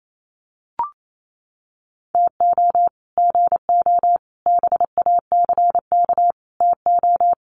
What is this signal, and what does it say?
Morse code, heard as a keyed tone, about 28 wpm, 700 Hz. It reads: TO GO BACK TO
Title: Sets of 5 Words 28wpm